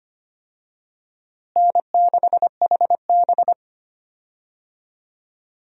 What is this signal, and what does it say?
Morse code, heard as a keyed tone, about 25 wpm, 700 Hz. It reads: N6HB
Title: USA Call Signs - 4 Characters - Repeated 2x - 25wpm